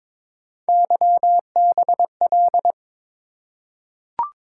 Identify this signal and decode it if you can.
Morse code, heard as a keyed tone, about 22 wpm, 700 Hz. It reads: YBL
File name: practice / 22wpm / Three Letter-Number Combinations - 22wpm